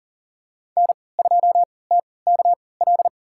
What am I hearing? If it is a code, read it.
Morse code, heard as a keyed tone, about 40 wpm, spaced as for 23 wpm, 700 Hz. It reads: N2TKL
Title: USA Call Signs - 5 Characters - 3x Character Spacing - 40wpm